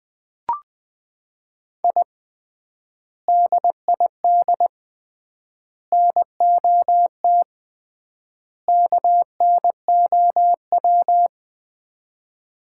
Morse code, heard as a keyed tone, about 21 wpm, 700 Hz. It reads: I DID NOT KNOW
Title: 3x Word-Spacing - Sets of 4 Words - 20wpm